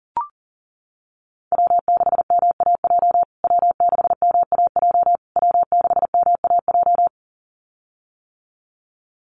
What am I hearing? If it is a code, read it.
Morse code, heard as a keyed tone, about 40 wpm, 700 Hz. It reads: W6MAJ W6MAJ W6MAJ
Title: USA Call Signs Repeated 3x 40wpm